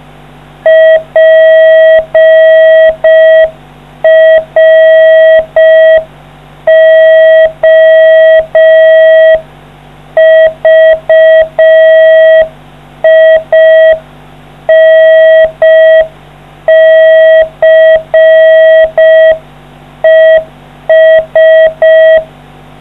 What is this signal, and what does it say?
Morse code, heard as a keyed tone, about 4 wpm, 640 Hz. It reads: PROVINCES